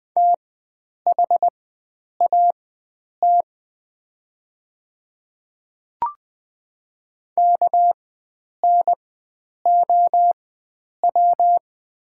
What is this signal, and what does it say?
Morse code, heard as a keyed tone, about 20 wpm, spaced as for 9 wpm, 690 Hz. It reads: THAT KNOW
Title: 4x Character Spacing - Top 500 Words - 20wpm